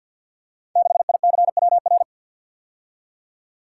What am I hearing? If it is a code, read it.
Morse code, heard as a keyed tone, about 52 wpm, 700 Hz. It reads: 6ICWR
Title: Code Groups - 5 Letter:Numbers - 50wpm